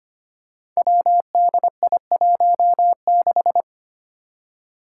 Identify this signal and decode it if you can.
Morse code, heard as a keyed tone, about 25 wpm, 700 Hz. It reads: WDI16
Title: Five Challenging Letter-Number Combinations 25wpm